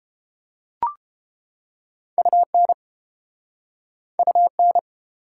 Morse code, heard as a keyed tone, about 30 wpm, 700 Hz. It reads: UN UN